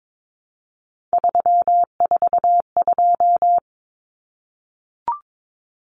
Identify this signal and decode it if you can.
Morse code, heard as a keyed tone, about 22 wpm, 700 Hz. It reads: 342
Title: Numbers - 3-Digit 22wpm